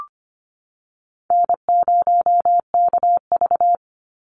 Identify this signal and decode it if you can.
Morse code, heard as a keyed tone, about 25 wpm, 700 Hz. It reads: N0KV
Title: Call Signs - 25wpm